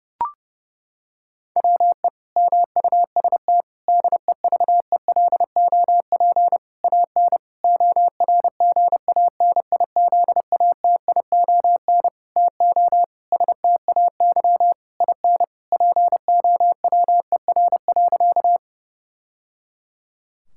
Morse code, heard as a keyed tone, about 30 wpm, 700 Hz. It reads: WE MUST DEVELOP AN ORGANIZATION TO STAY IN POWER.
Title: Sentences from Top 1000 Words - Review Part 5 - 30wpm